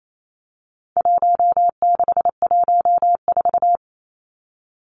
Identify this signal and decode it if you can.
Morse code, heard as a keyed tone, about 28 wpm, 700 Hz. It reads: 1614
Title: Numbers - Contest Style - 4-Digit 28wpm